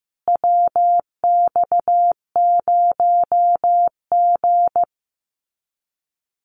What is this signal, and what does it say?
Morse code, heard as a keyed tone, about 15 wpm, 700 Hz. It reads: WX0G
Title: Common QSO Phrases Encore 15wpm